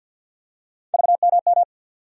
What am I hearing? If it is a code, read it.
Morse code, heard as a keyed tone, about 51 wpm, 700 Hz. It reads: VMM
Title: USA Call Sign Postfixes - 3 Letters 50wpm